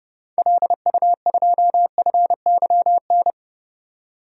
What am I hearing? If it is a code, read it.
Morse code, heard as a keyed tone, about 32 wpm, 700 Hz. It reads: LU2FYN
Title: DXCC-100 Call Signs - 6 Characters - 30wpm